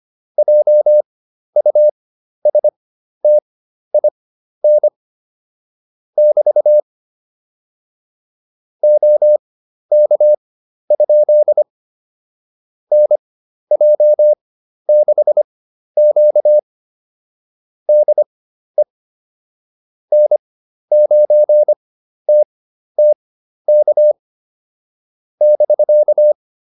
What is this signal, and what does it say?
Morse code, heard as a keyed tone, about 25 wpm, spaced as for 12 wpm, 600 Hz. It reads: JUSTIN <BT> OK? NJ6Q DE N9TTK <BK>